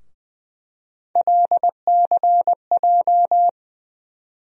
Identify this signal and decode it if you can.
Morse code, heard as a keyed tone, about 20 wpm, 700 Hz. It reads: LCJ
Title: USA Call Sign Postfixes - 3 Letters 20wpm